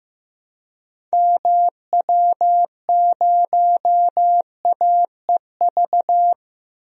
Morse code, heard as a keyed tone, about 15 wpm, 700 Hz. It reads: MW0AEV
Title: DXCC-100 Call Signs - 6 Characters - 15wpm